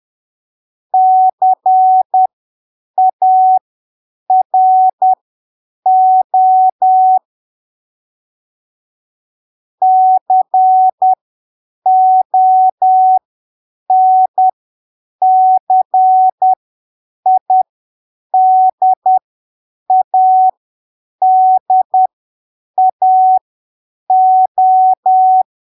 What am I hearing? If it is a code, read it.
Morse code, heard as a keyed tone, about 10 wpm, spaced as for 7 wpm, 750 Hz. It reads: CARO CONCIDADAO
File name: arquivos / texto / texto 26 10ppm